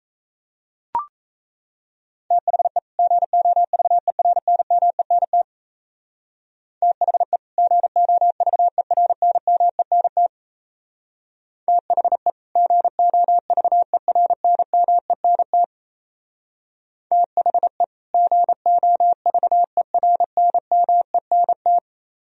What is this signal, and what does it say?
Morse code, heard as a keyed tone, about 33 wpm, 700 Hz. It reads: THE GOVERNMENT THE GOVERNMENT THE GOVERNMENT THE GOVERNMENT
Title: Speed-Racing - Sets of 2 Words 28wpm